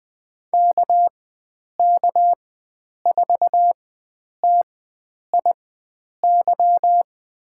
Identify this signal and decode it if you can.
Morse code, heard as a keyed tone, about 20 wpm, spaced as for 9 wpm, 700 Hz. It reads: KK4TIY
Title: USA Call Signs - 6 Characters - 4x Character Spacing - 20wpm